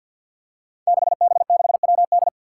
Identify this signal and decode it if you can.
Morse code, heard as a keyed tone, about 47 wpm, 700 Hz. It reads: 6B6WD
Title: Five Challenging Letter-Number Combinations 50wpm